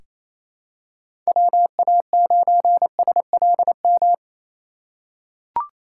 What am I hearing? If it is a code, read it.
Morse code, heard as a keyed tone, about 27 wpm, 700 Hz. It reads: WA9SLM
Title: USA Call Signs - 6 Characters - 3x Character Spacing - 28wpm